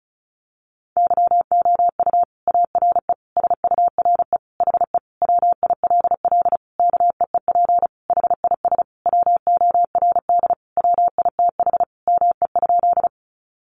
Morse code, heard as a keyed tone, about 34 wpm, 700 Hz. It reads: YOU ARE SURE HE WILL KEEP HIS WORD WITH ME?
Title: Sentences from Top 300 Words 35wpm